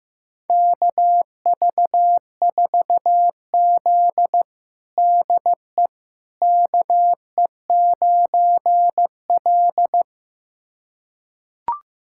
Morse code, heard as a keyed tone, about 15 wpm, 700 Hz. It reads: KV4Z DE KE9L
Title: USA Callsign DE Callsign - 4 Characters - 15wpm